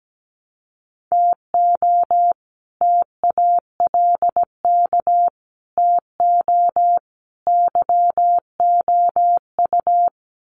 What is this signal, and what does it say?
Morse code, heard as a keyed tone, about 17 wpm, 700 Hz. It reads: TO TALK TO YOU
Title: Sets of 5 Words Encore 17wpm